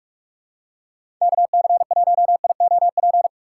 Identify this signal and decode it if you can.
Morse code, heard as a keyed tone, about 45 wpm, 700 Hz. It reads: KC1IOP
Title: USA Call Signs - 6 Characters - 8x Character Spacing - 45wpm